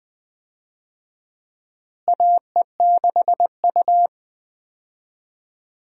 Morse code, heard as a keyed tone, about 20 wpm, 700 Hz. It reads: AE6U
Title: USA Call Signs - 4 Characters - Repeated 2x - 20wpm